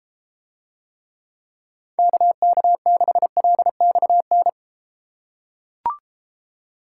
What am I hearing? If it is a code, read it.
Morse code, heard as a keyed tone, about 33 wpm, 700 Hz. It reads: KK6LXN